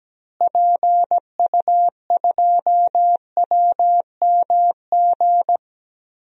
Morse code, heard as a keyed tone, about 17 wpm, 700 Hz. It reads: PU2WMG
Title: DXCC-100 Call Signs - 6 Characters - Repeated 2x - 17wpm